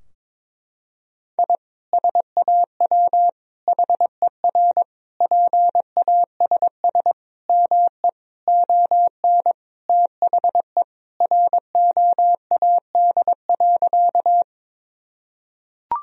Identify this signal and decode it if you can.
Morse code, heard as a keyed tone, about 22 wpm, 700 Hz. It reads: I SAW HER PASS ME ON THE ROAD.
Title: Sentences from Top 1000 Words - Review Part 2 - 22wpm